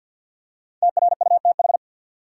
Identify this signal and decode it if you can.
Morse code, heard as a keyed tone, about 50 wpm, 700 Hz. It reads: TRUTH
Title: Five Letter Words - 50wpm